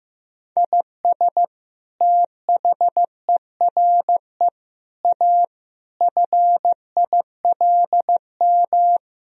Morse code, heard as a keyed tone, about 16 wpm, 700 Hz. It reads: IS THERE A FILM
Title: Sentences from Top 1000 Words - Review Part 3 - 15wpm